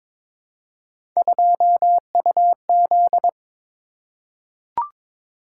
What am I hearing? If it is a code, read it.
Morse code, heard as a keyed tone, about 22 wpm, 700 Hz. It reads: 2UZ
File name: practice / 22wpm / Three Letter-Number Combinations - 22wpm